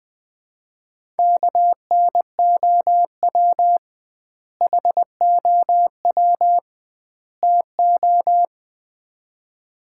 Morse code, heard as a keyed tone, about 20 wpm, 700 Hz. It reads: KNOW HOW TO
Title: 2x Word-Spacing - Sets of 3 Words - 20wpm